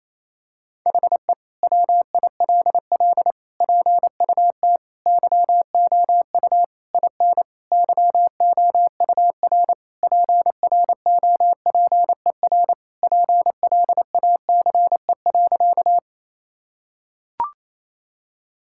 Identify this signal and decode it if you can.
Morse code, heard as a keyed tone, about 28 wpm, 700 Hz. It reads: HE WILL PUT YOU IN YOUR PROPER PLACE.
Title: Sentences from Top 1000 Words - Review Part 5 - 28wpm